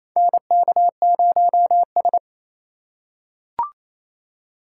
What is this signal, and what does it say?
Morse code, heard as a keyed tone, about 27 wpm, 700 Hz. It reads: NK0S
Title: USA Call Signs - 4 Characters - 4x Character Spacing - 28wpm